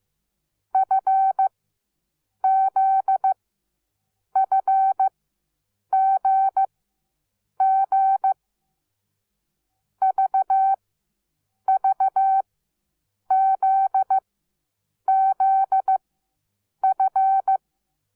Morse code, heard as a keyed tone, about 15 wpm, spaced as for 7 wpm, 770 Hz. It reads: FZFGG VVZZF
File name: Onlinelehrgang / morsen / DJ4UF Morsen17